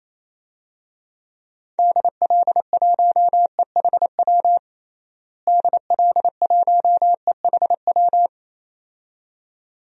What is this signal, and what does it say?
Morse code, heard as a keyed tone, about 28 wpm, 700 Hz. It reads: DL1EHW DL1EHW
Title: DXCC-100 Call Signs - 6 Characters - Repeated 2x - 28wpm